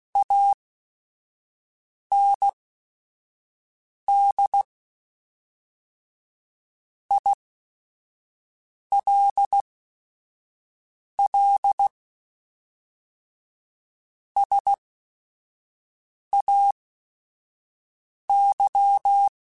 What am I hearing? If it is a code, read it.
Morse code, heard as a keyed tone, about 16 wpm, spaced as for 5 wpm, 800 Hz. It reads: AND ILL SAY